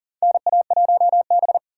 Morse code, heard as a keyed tone, about 38 wpm, 700 Hz. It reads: NA1B
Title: USA Call Signs - 4 Characters - 2x Character Spacing - 40wpm